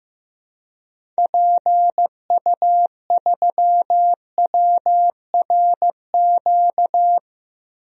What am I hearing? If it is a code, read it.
Morse code, heard as a keyed tone, about 15 wpm, 700 Hz. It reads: PU3WRQ
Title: DXCC-100 Call Signs - 6 Characters - Repeated 2x - 15wpm